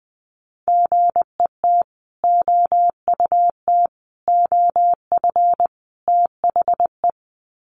Morse code, heard as a keyed tone, about 20 wpm, 700 Hz. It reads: GET OUT OF THE
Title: Sets of 4 Words 20wpm